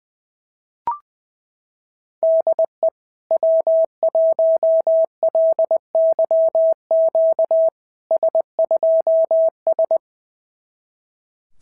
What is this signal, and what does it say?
Morse code, heard as a keyed tone, about 21 wpm, 650 Hz. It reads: DE W1LYQ S2S
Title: SOTA QSO Practice 20wpm